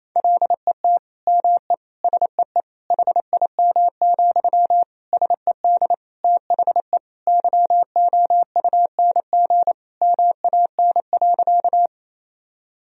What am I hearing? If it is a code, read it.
Morse code, heard as a keyed tone, about 28 wpm, 700 Hz. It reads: LET ME SEE HIM, SED THE YOUNG MAN.
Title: Sentences from Top 400 Words and Abbreviations 28wpm